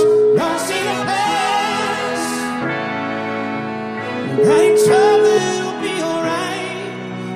Ringback tone (Europe, 425 Hz), with noise.